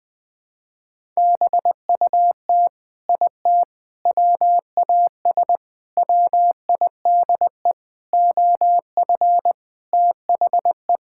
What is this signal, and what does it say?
Morse code, heard as a keyed tone, about 20 wpm, 700 Hz. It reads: BUT IT WAS WIDE OF THE
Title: Sentences from Top 1000 Words - Review Part 6 - 20wpm